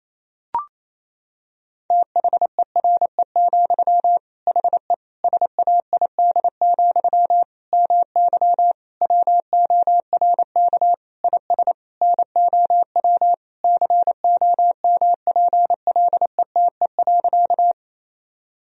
Morse code, heard as a keyed tone, about 28 wpm, 700 Hz. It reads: THERE, HE SAID, MY WORK IS NOW COMPLETE.
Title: Sentences from Top 1000 Words - Review Part 2 - 28wpm